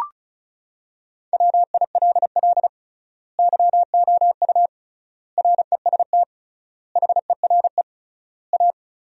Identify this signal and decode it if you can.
Morse code, heard as a keyed tone, about 36 wpm, 700 Hz. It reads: WILL YOU REST HERE A
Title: 3x Word-Spacing - Sentences from Top 500 Words - 35wpm